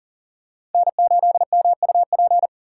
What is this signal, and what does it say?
Morse code, heard as a keyed tone, about 40 wpm, 700 Hz. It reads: N8MUP
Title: USA Call Signs - 5 Characters - 3x Character Spacing - 40wpm